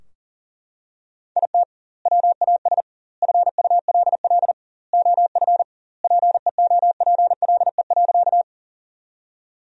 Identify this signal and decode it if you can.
Morse code, heard as a keyed tone, about 40 wpm, 700 Hz. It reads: IT WAS FULL OF PEOPLE.
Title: 2x Word-Spacing - Sentences from Top 500 Words - 40wpm